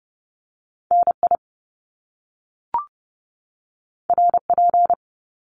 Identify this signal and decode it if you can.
Morse code, heard as a keyed tone, about 30 wpm, 700 Hz. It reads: NI RP
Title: Top Two Letter Combinations in Words 30wpm